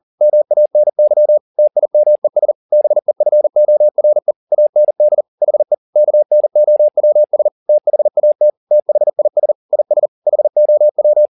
Morse code, heard as a keyed tone, about 40 wpm, 600 Hz. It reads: MANY TIMES BEFORE AND HE KNOWS THAT THIS IS HOW